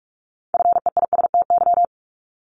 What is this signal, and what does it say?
Morse code, heard as a keyed tone, about 46 wpm, 700 Hz. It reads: FEISTY
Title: Multi-letter Phonemes Repeated 3x 45wpm